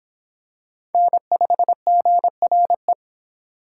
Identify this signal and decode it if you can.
Morse code, heard as a keyed tone, about 24 wpm, 700 Hz. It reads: N5GRE